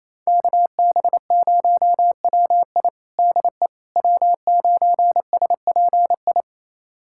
Morse code, heard as a keyed tone, about 27 wpm, 700 Hz. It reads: KB0WI DE W9SPI
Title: USA Callsign DE Callsign - 5 Characters - 28wpm